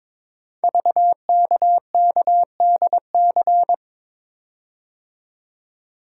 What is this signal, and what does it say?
Morse code, heard as a keyed tone, about 23 wpm, 700 Hz. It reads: VKKDC
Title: Code Groups - 5 Letters - 22wpm